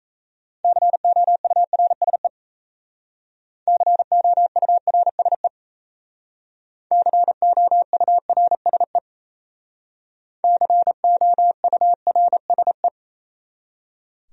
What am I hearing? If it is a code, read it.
Morse code, heard as a keyed tone, about 36 wpm, 700 Hz. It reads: COURSE COURSE COURSE COURSE